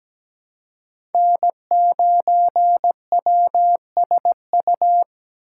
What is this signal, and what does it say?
Morse code, heard as a keyed tone, about 17 wpm, 700 Hz. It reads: N9WSU
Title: USA Call Signs - 6x Character Spacing - 17wpm